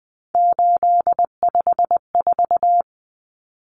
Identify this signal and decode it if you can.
Morse code, heard as a keyed tone, about 20 wpm, 700 Hz. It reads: 854
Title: Common Cut Numbers - 3-Digit 20wpm